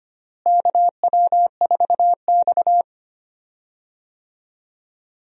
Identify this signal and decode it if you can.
Morse code, heard as a keyed tone, about 25 wpm, 700 Hz. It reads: KW4X
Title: USA Call Signs - 4 Characters - Repeated 2x - 25wpm